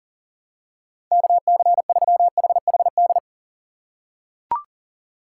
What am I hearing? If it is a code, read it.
Morse code, heard as a keyed tone, about 40 wpm, 700 Hz. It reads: KC3HHD
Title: USA Call Signs - 6 Characters - 4x Character Spacing - 40wpm